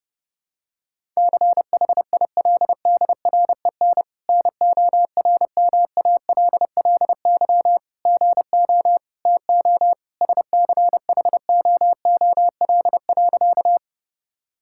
Morse code, heard as a keyed tone, about 30 wpm, 700 Hz. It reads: CHILDREN NORMALLY GO TO SCHOOL.
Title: Sentences from Top 1000 Words - Review Part 6 - 30wpm